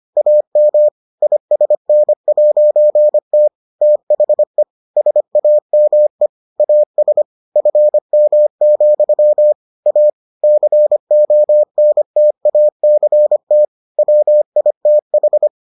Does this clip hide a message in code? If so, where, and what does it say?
Yes: Morse code from 0.2 to 15.5 s, heard as a keyed tone, about 25 wpm, 600 Hz. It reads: AM ISN'T THE SAME AS FM, A CONTACT WITH